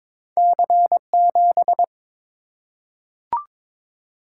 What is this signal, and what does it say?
Morse code, heard as a keyed tone, about 23 wpm, 690 Hz. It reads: C7
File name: practice / 22wpm / Two Letter-Number Combinations - 22wpm